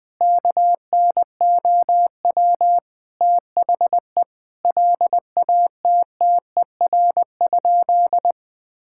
Morse code, heard as a keyed tone, about 20 wpm, 700 Hz. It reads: KNOW THE LATTER?
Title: Sentences from Top 1600 Words 20wpm